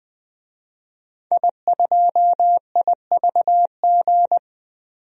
Morse code, heard as a keyed tone, about 21 wpm, 700 Hz. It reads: I2IVG
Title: DXCC-100 Call Signs - 5 Characters - 20wpm